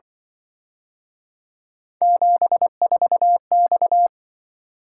Morse code, heard as a keyed tone, about 24 wpm, 700 Hz. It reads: 74X
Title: Speed-Racing - Three Letter-Number Combinations - 20wpm